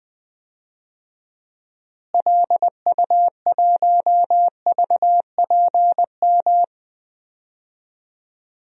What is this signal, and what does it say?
Morse code, heard as a keyed tone, about 20 wpm, 700 Hz. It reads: LU1VPM
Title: DXCC-100 Call Signs - 6 Characters - Repeated 2x - 20wpm